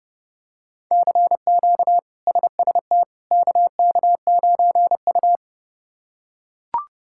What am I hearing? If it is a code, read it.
Morse code, heard as a keyed tone, about 30 wpm, 700 Hz. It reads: CQ SST KK9U
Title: K1USN SST QSO Practice - 30wpm